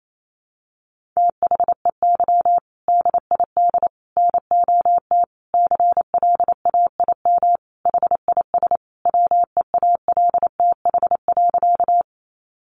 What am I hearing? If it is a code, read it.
Morse code, heard as a keyed tone, about 28 wpm, 700 Hz. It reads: THEY DID NOT CLAIM HIS WEALTH.
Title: Sentences from Top 1500 Words 28wpm